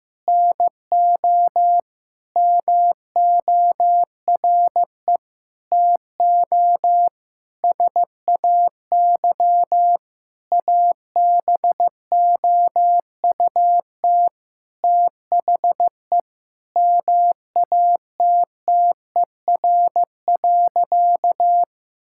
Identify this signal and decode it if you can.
Morse code, heard as a keyed tone, about 15 wpm, 700 Hz. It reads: NO MORE TO SAY ABOUT THE MATTER.
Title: Sentences from Top 1000 Words - Review Part 5 - 15wpm